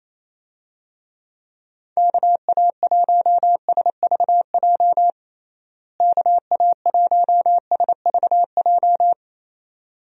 Morse code, heard as a keyed tone, about 28 wpm, 700 Hz. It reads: KA1SVJ KA1SVJ
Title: USA Call Signs - 6 Characters - Repeated 2x - 28wpm